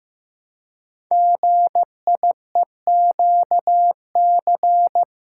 Morse code, heard as a keyed tone, about 15 wpm, 700 Hz. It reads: GIEQC
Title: Code Groups - 5 Letter:Numbers - 15wpm